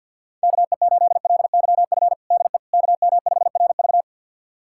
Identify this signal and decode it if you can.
Morse code, heard as a keyed tone, about 52 wpm, 700 Hz. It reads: KE8LYF DE KM5RV
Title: USA Callsign DE Callsign 50wpm